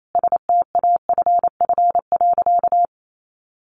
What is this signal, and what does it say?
Morse code, heard as a keyed tone, about 28 wpm, 700 Hz. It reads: STAFF.
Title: Sentences from Top 1200 Words 28wpm